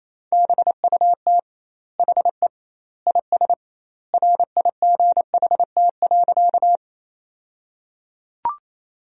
Morse code, heard as a keyed tone, about 28 wpm, 700 Hz. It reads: BUT HE IS RIGHT.
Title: 2x Word-Spacing - Sentences from Top 200 Words - 28wpm